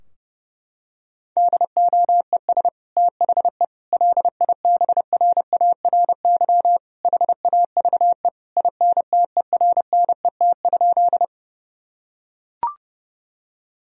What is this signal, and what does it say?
Morse code, heard as a keyed tone, about 30 wpm, 700 Hz. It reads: DOES THE LIBRARY HAVE INTERNET?
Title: Sentences from Top 1000 Words - Review Part 4 - 30wpm